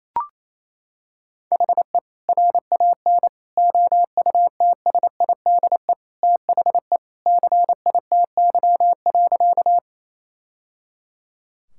Morse code, heard as a keyed tone, about 28 wpm, 700 Hz. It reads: HE RAN OUTSIDE THE CITY.
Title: Sentences from Top 1000 Words - Review Part 6 - 28wpm